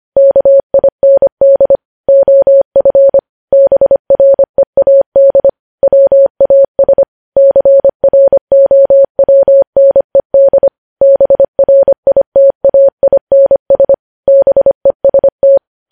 Morse code, heard as a keyed tone, about 25 wpm, 550 Hz. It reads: KIND OF BREAD WAS CROWNED BRITAINS BEST
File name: podcast / News headlines-1x-25x25